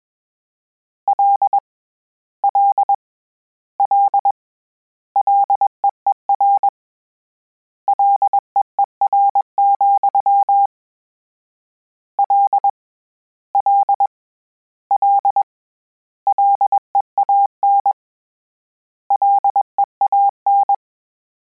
Morse code, heard as a keyed tone, about 21 wpm, 800 Hz. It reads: L L L LEER LEER, L L L LEAN LEAN
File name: Combinations / Combinations-mp3s / Combinations@20